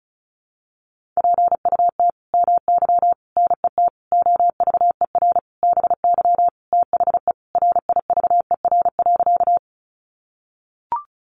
Morse code, heard as a keyed tone, about 34 wpm, 700 Hz. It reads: PUT MY NET OVER BY THE RIVER.
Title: Sentences from Top 1400 Words 35wpm